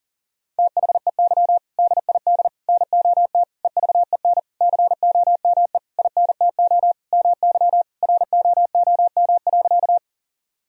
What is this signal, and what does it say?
Morse code, heard as a keyed tone, about 40 wpm, 700 Hz. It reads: THEY DID NOT EVEN COME INTO MY ROOM.
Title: Sentences from Top 1000 Words - Review Part 1 - 40wpm